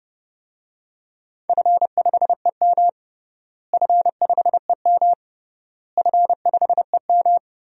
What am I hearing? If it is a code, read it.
Morse code, heard as a keyed tone, about 30 wpm, 700 Hz. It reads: F5EM F5EM F5EM
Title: DXCC-100 Call Signs - 4 Characters - Repeated 3x - 30wpm